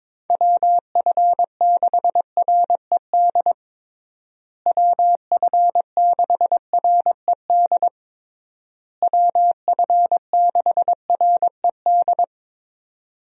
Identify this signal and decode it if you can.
Morse code, heard as a keyed tone, about 22 wpm, 700 Hz. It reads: WF6RED WF6RED WF6RED
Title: USA Call Signs - 6 Characters - Repeated 3x - 22wpm